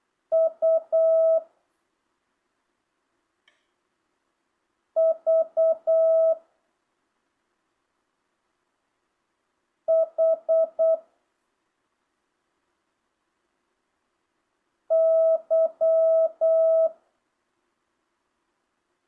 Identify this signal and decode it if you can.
Morse code, heard as a keyed tone, about 8 wpm, 650 Hz. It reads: U V H Y
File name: jh8cbh / 2025cw